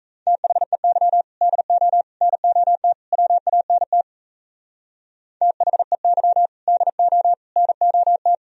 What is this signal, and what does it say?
Morse code, heard as a keyed tone, about 35 wpm, 700 Hz. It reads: THEY DO NOT WANT THEY DO NOT
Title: Speed-Racing - Sets of 4 Words 28wpm